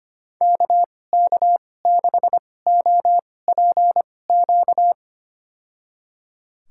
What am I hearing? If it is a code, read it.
Morse code, heard as a keyed tone, about 25 wpm, spaced as for 18 wpm, 700 Hz. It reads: KK6OPQ
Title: USA Call Signs - 6 Characters - 2x Character Spacing - 25wpm